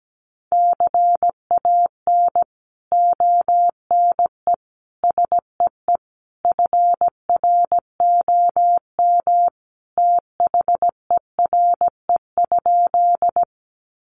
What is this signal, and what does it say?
Morse code, heard as a keyed tone, about 17 wpm, 700 Hz. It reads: CAN ONE SEE FROM THERE?
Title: Sentences from Top 100 Words 17wpm